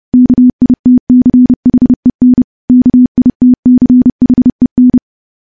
Morse code, heard as a keyed tone, about 30 wpm, 250 Hz. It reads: KITCHEN KITCHEN
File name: morse / morse-training-30WPM